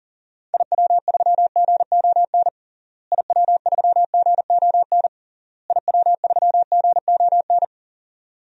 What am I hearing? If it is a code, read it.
Morse code, heard as a keyed tone, about 37 wpm, 700 Hz. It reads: IW3GON IW3GON IW3GON
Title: DXCC-100 Call Signs - 6 Characters - Repeated 3x - 40wpm